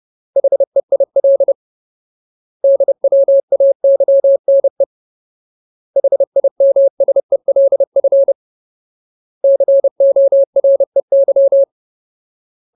Morse code, heard as a keyed tone, about 30 wpm, 550 Hz. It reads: HEIL DWAYNE HIMSELF COREY
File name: Ultimate Word List / 30WPM Ultimate